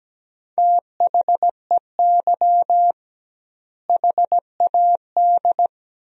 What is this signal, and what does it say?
Morse code, heard as a keyed tone, about 17 wpm, 700 Hz. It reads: THEY HAD
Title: 2x Word-Spacing - Sentences from Top 400 Words - 17wpm